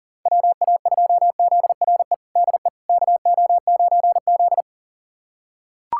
Morse code, heard as a keyed tone, about 40 wpm, 700 Hz. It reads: WA2ZRE DE KO9Z
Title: USA Callsign DE Callsign 40wpm